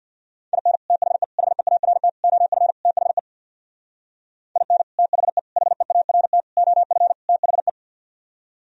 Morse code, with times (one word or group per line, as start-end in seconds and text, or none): 0.5-0.8 IN
0.9-1.2 THE
1.4-2.1 HEART
2.2-2.7 OF
2.9-3.2 THE
4.6-4.8 IN
5.0-5.4 THE
5.6-6.4 HEART
6.6-7.1 OF
7.3-7.7 THE